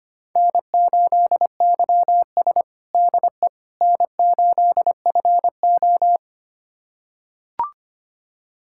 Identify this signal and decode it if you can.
Morse code, heard as a keyed tone, about 25 wpm, 700 Hz. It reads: N8YS DE N8FO
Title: USA Callsign DE Callsign - 4 Characters - 25wpm